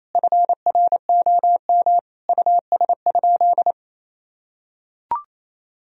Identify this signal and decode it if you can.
Morse code, heard as a keyed tone, about 28 wpm, 700 Hz. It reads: FROM US?
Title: Sentences from Top 1000 Words - Review Part 4 - 28wpm